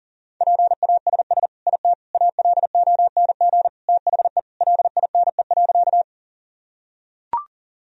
Morse code, heard as a keyed tone, about 40 wpm, 700 Hz. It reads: PASS IT ALONG THE LINE.
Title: Sentences from Top 1000 Words - Review Part 1 - 40wpm